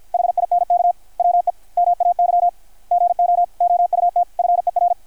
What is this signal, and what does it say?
Morse code, heard as a keyed tone, about 51 wpm, 660 Hz. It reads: HINK WE MAY GO OUT FER
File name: archived / CRSnet